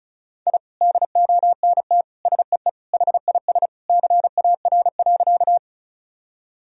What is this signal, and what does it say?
Morse code, heard as a keyed tone, about 34 wpm, 700 Hz. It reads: I DONT SEE HIS CAR.
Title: Sentences from Top 1000 Words - Review Part 1 - 35wpm